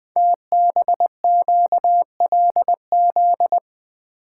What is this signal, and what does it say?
Morse code, heard as a keyed tone, about 20 wpm, 700 Hz. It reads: TBQLZ
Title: Code Groups - 5 Letter:Numbers - 20wpm